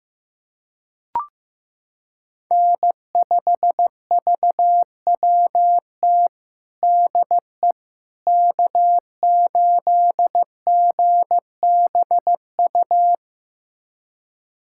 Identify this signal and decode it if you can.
Morse code, heard as a keyed tone, about 15 wpm, 700 Hz. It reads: N5VWT DE K8GBU
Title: USA Callsign DE Callsign - 5 Characters - 15wpm